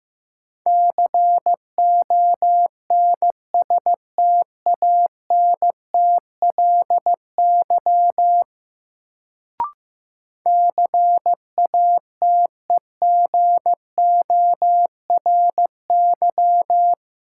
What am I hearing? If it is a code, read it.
Morse code, heard as a keyed tone, about 16 wpm, 700 Hz. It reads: CONSTANTLY CATEGORY
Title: Sentences from Top 1000 Words - Review Part 7 - New Word Drill - 15wpm